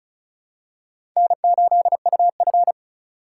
Morse code, heard as a keyed tone, about 34 wpm, 700 Hz. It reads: N8UF
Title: USA Call Signs - 4 Characters - 6x Character Spacing - 35wpm